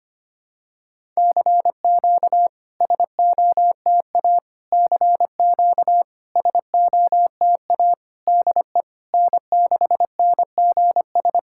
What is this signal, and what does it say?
Morse code, heard as a keyed tone, about 25 wpm, 700 Hz. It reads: CQ SOTA CQ SOTA DE N6NGS
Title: SOTA QSO Practice 25wpm